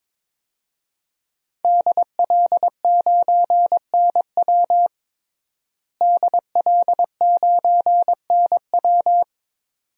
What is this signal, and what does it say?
Morse code, heard as a keyed tone, about 22 wpm, 700 Hz. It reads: DL9NW DL9NW
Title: DXCC-100 Call Signs - 5 Characters - Repeated 2x - 22wpm